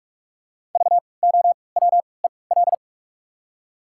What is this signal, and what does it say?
Morse code, heard as a keyed tone, about 46 wpm, 700 Hz. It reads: V O W E L
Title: 3x Character Spacing - Top 500 Words - 45wpm